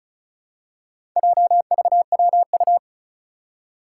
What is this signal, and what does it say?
Morse code, heard as a keyed tone, about 36 wpm, 700 Hz. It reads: JVWU
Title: Four Challenging Letter-Number Combinations 35wpm